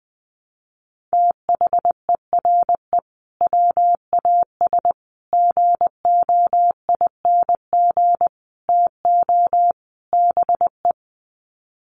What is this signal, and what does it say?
Morse code, heard as a keyed tone, about 20 wpm, 700 Hz. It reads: THERE WAS GOING TO BE
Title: Sets of 5 Words Encore2 20wpm